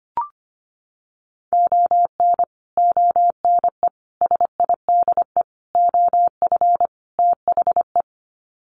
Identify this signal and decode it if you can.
Morse code, heard as a keyed tone, about 25 wpm, 700 Hz. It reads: ON ONE SIDE OF THE
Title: Sets of 5 Words 25wpm